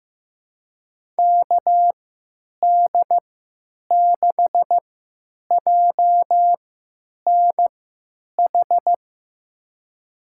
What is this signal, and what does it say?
Morse code, heard as a keyed tone, about 15 wpm, spaced as for 9 wpm, 700 Hz. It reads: KD6JNH
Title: USA Call Signs - 6 Characters - 3x Character Spacing - 15wpm